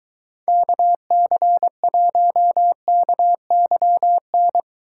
Morse code, heard as a keyed tone, about 22 wpm, 700 Hz. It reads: KC1KYN